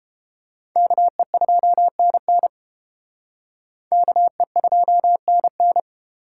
Morse code, heard as a keyed tone, about 29 wpm, 700 Hz. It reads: KE2NN KE2NN